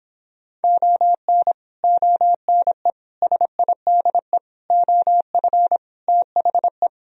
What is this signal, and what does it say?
Morse code, heard as a keyed tone, about 26 wpm, 700 Hz. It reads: ON ONE SIDE OF THE